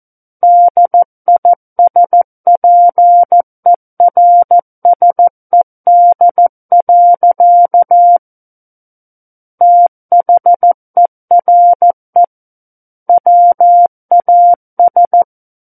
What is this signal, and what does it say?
Morse code, heard as a keyed tone, about 14 wpm, 700 Hz. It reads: DISPERSED. THERE WAS
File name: Book One / 05-Chapter 5 - 14 WPM